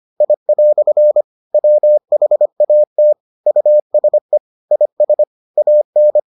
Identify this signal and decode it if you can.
Morse code, heard as a keyed tone, about 25 wpm, 600 Hz. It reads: I" WHAT USE IS AN